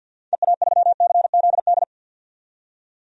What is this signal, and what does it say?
Morse code, heard as a keyed tone, about 48 wpm, 700 Hz. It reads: EA3CZD
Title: DXCC-100 Call Signs - 6 Characters - 50wpm